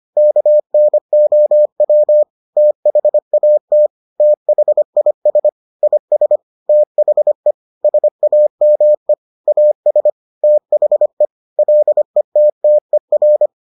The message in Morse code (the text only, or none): KNOW THAT THIS IS THE SAME AS THE LETTER